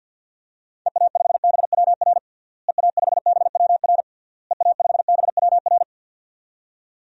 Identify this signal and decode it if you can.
Morse code, heard as a keyed tone, about 49 wpm, 700 Hz. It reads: EA5BWR EA5BWR EA5BWR
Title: DXCC-100 Call Signs - 6 Characters - Repeated 3x - 50wpm